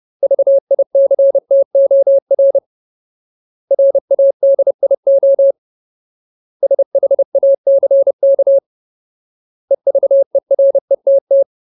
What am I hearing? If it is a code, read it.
Morse code, heard as a keyed tone, about 30 wpm, 550 Hz. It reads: VICTOR RADIO SHACK EVERETT
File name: Ultimate Word List / 30WPM Ultimate